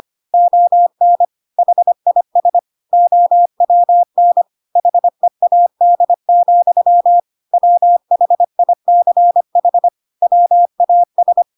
Morse code, heard as a keyed tone, about 25 wpm, 700 Hz. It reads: ON HIS OWN HEAD, WHICH WAS
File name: Pinocchio / Pinocchio ch3 25 25 ik1hge